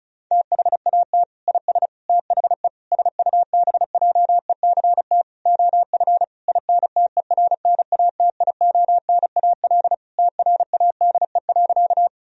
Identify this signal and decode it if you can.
Morse code, heard as a keyed tone, about 34 wpm, 700 Hz. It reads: THAT IS THE SUBJECT OF INTERNATIONAL TRADE.
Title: Sentences from Top 1000 Words - Review Part 4 - 35wpm